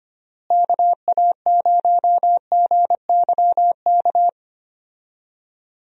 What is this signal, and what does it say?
Morse code, heard as a keyed tone, about 25 wpm, 700 Hz. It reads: KA0GYK
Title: USA Call Signs - 6 Characters - Repeated 2x - 25wpm